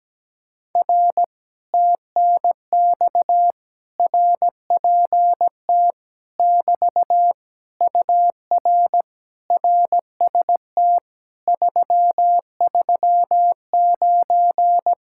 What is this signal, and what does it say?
Morse code, heard as a keyed tone, about 17 wpm, 700 Hz. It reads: R TNX RPT <BT> UR RST 339
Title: Speed-Racing Common QSO Phrases Encore - 17wpm